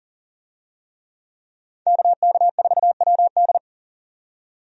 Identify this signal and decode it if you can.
Morse code, heard as a keyed tone, about 40 wpm, 700 Hz. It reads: KK4WD
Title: USA Call Signs - 5 Characters - 40wpm